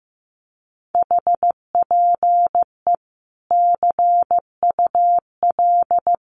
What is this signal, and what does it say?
Morse code, heard as a keyed tone, about 15 wpm, 700 Hz. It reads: HPE CUL
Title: Common QSO Phrases 15wpm